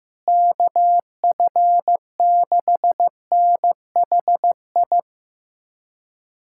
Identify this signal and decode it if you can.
Morse code, heard as a keyed tone, about 15 wpm, 700 Hz. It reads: KF6NHI
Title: USA Callsign DE Callsign 15wpm